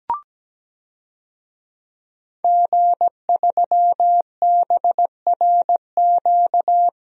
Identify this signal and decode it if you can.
Morse code, heard as a keyed tone, about 17 wpm, 700 Hz. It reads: G3BRQ
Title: DXCC-100 Call Signs - 5 Characters - Repeated 2x - 17wpm